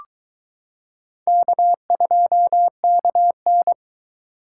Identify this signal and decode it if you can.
Morse code, heard as a keyed tone, about 22 wpm, 700 Hz. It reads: K2KN